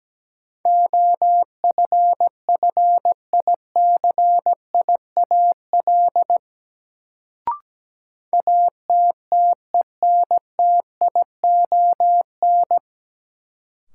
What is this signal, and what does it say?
Morse code, heard as a keyed tone, about 17 wpm, 700 Hz. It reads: OFFICIAL ATTENTION
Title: Sentences from Top 1000 Words - New Word Drill - 17wpm